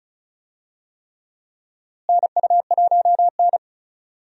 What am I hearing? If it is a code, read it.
Morse code, heard as a keyed tone, about 34 wpm, 700 Hz. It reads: NU1N
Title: USA Call Signs - 4 Characters - Repeated 3x - 35wpm